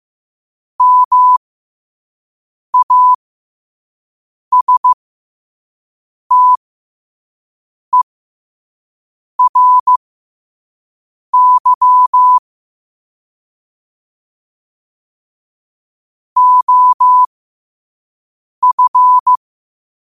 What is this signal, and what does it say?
Morse code, heard as a keyed tone, about 15 wpm, spaced as for 5 wpm, 1000 Hz. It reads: MASTERY OF
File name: qotdfiles / 5wpm